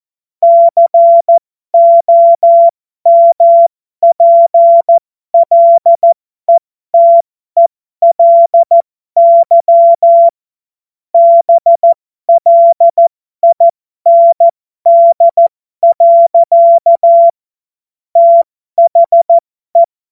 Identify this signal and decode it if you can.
Morse code, heard as a keyed tone, about 14 wpm, spaced as for 12 wpm, 680 Hz. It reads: COMPLETELY BLIND. THE